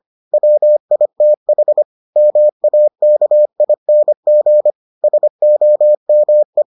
Morse code, heard as a keyed tone, about 25 wpm, 600 Hz. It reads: WITH MAKING SOME